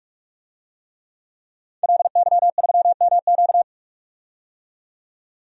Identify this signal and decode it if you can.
Morse code, heard as a keyed tone, about 43 wpm, 700 Hz. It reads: LY3MQ
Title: DXCC-100 Call Signs - 5 Characters - 45wpm